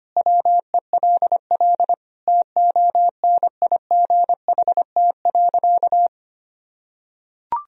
Morse code, heard as a keyed tone, about 25 wpm, 700 Hz. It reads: WELL TONIGHT.
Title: Sentences from Top 1000 Words - Review Part 4 - 25wpm